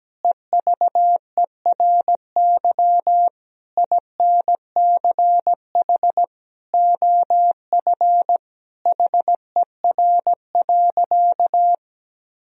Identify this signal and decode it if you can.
Morse code, heard as a keyed tone, about 17 wpm, 700 Hz. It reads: EVERY INCH OF HER.
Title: Sentences from Top 1000 Words - Review Part 3 - 17wpm